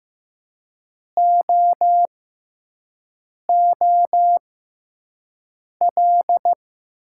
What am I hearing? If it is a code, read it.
Morse code, heard as a keyed tone, about 15 wpm, spaced as for 5 wpm, 700 Hz. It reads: OOL
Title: 6x Character Spacing - Top 500 Words - 15wpm